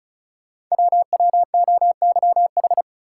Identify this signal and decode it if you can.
Morse code, heard as a keyed tone, about 36 wpm, 690 Hz. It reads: WWOYH
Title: Code Groups - 5 Letters - 35wpm